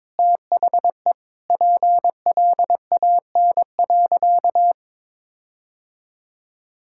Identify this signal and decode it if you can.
Morse code, heard as a keyed tone, about 22 wpm, 700 Hz. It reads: THE PLAN.
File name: practice / 22wpm / Sentences with 4-Word Phrases - 22wpm